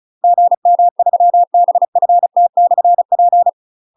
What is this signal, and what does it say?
Morse code, heard as a keyed tone, about 35 wpm, 700 Hz. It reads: GM3BFT/P
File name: UK MASTER QSOs TRAINING / 35WPM / UK MASTER QSOs 0013